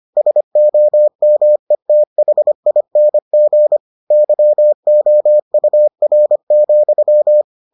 Morse code, heard as a keyed tone, about 25 wpm, 600 Hz. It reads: SOMETHING YOUR,